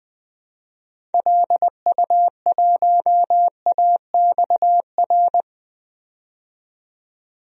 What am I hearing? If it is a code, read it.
Morse code, heard as a keyed tone, about 20 wpm, 700 Hz. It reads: LU1AXR
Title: DXCC-100 Call Signs - 6 Characters - 20wpm